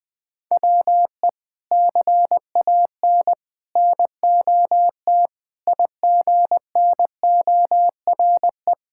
Morse code, heard as a keyed tone, about 20 wpm, 700 Hz. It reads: WE CAN NOT IGNORE
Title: Sentences with 3-Word Phrases - Spoken Beginning - 20wpm